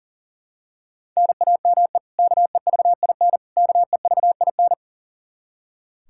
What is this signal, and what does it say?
Morse code, heard as a keyed tone, about 37 wpm, 700 Hz. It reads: NAME KEVIN KEVIN
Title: Practice QSO - Exchange 1 - Beginning - 40wpm